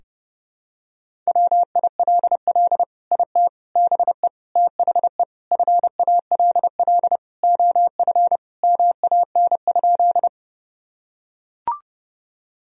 Morse code, heard as a keyed tone, about 32 wpm, 700 Hz. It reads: WILL IT BE THE FALL OF MAN?
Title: Sentences from Top 1000 Words - Review Part 2 - 30wpm